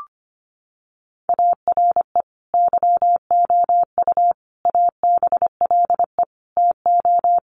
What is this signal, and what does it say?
Morse code, heard as a keyed tone, about 25 wpm, 700 Hz. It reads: ARE YOU ABLE TO
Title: Sentences from Top 1500 Words 25wpm